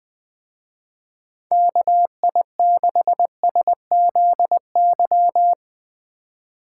Morse code, heard as a keyed tone, about 20 wpm, 700 Hz. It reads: KI6SZY
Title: Speed Racing - USA 2x3 Call Signs 17wpm